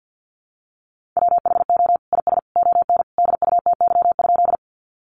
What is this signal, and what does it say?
Morse code, heard as a keyed tone, about 50 wpm, 700 Hz. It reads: WHO IS ON DUTY?
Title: Sentences from Top 1500 Words 50wpm